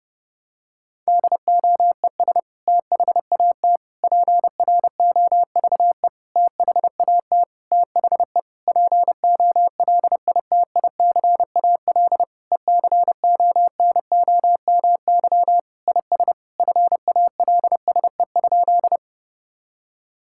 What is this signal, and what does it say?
Morse code, heard as a keyed tone, about 30 wpm, 700 Hz. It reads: DOES THAT PROVE THAT THE POLITICAL ECONOMY IS FALSE?
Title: Sentences from Top 1600 Words 30wpm